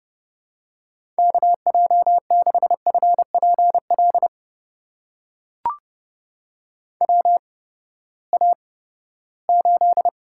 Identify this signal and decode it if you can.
Morse code, heard as a keyed tone, about 30 wpm, spaced as for 17 wpm, 700 Hz. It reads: KJ6FPL WA8
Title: USA Call Signs - 6 Characters - 8x Character Spacing - 30wpm